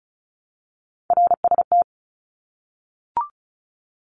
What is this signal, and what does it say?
Morse code, heard as a keyed tone, about 34 wpm, 700 Hz. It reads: RST
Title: Abbreviations 35wpm